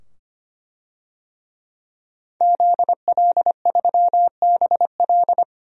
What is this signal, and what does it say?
Morse code, heard as a keyed tone, about 25 wpm, 700 Hz. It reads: ZL3BL
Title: DXCC-100 Call Signs - 5 Characters - 25wpm